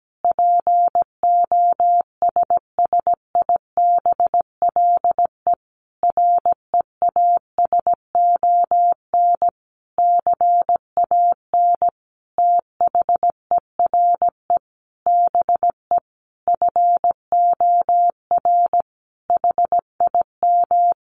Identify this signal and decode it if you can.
Morse code, heard as a keyed tone, about 17 wpm, 700 Hz. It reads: POSSIBLE REASON CAN THERE BE FOR HIM
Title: Sentences from Top 1300 Words 17wpm